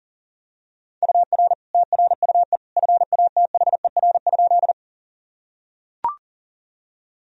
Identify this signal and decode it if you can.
Morse code, heard as a keyed tone, about 40 wpm, 700 Hz. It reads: UR TRUE FATHER?
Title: Sentences from Top 400 Words and Abbreviations 40wpm